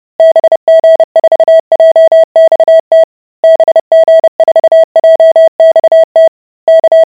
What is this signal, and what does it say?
Morse code, heard as a keyed tone, about 30 wpm, 650 Hz. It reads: BG4JXT BG4JXT K